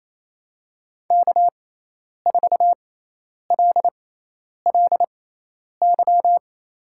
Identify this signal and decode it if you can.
Morse code, heard as a keyed tone, about 28 wpm, spaced as for 10 wpm, 700 Hz. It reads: K4LLY
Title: USA Call Signs - 6x Character Spacing - 28wpm